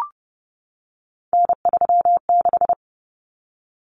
Morse code, heard as a keyed tone, about 30 wpm, 700 Hz. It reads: N36
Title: Common Cut Numbers - 3-Digit 30wpm